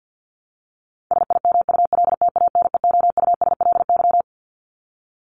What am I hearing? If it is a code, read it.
Morse code, heard as a keyed tone, about 50 wpm, 700 Hz. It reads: SIMULTANEOUSLY
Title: LY Suffix - Top 100 Words with Base Words 50wpm